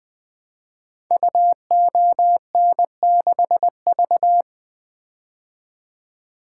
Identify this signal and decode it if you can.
Morse code, heard as a keyed tone, about 20 wpm, 700 Hz. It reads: UON6V
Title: Code Groups - 5 Letter:Numbers - 20wpm